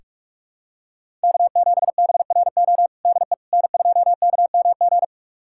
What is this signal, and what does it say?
Morse code, heard as a keyed tone, about 41 wpm, 700 Hz. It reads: K7BRO DE N2KMG
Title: USA Callsign DE Callsign - 5 Characters - 45wpm